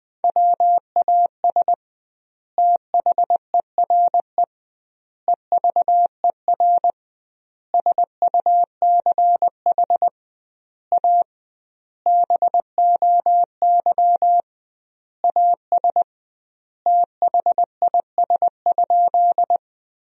Morse code, heard as a keyed tone, about 20 wpm, 700 Hz. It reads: WAS THERE EVER SUCH A BOY AS THIS?
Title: 2x Word-Spacing - Sentences from Top 400 Words - 20wpm